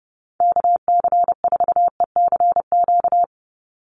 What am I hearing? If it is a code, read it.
Morse code, heard as a keyed tone, about 30 wpm, 700 Hz. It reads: KC4ECQ
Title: USA Call Signs 30wpm